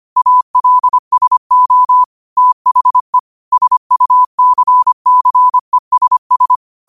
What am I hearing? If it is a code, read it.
Morse code, heard as a keyed tone, about 25 wpm, 1000 Hz. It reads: ALSO THE SUCCESS
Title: 25 WPM morse code quotes for Thu, 29 Feb 2024